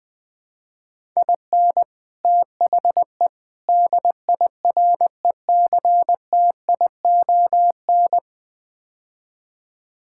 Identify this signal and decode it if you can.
Morse code, heard as a keyed tone, about 20 wpm, 700 Hz. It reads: IN THE DIRECTION
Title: Build A Phrase - 5-Word Phrases - 20wpm